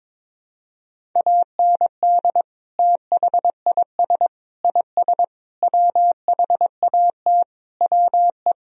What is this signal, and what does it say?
Morse code, heard as a keyed tone, about 21 wpm, 700 Hz. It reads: AND THIS IS WHAT WE
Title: Sentences from Top 1000 Words - Review Part 3 - 22wpm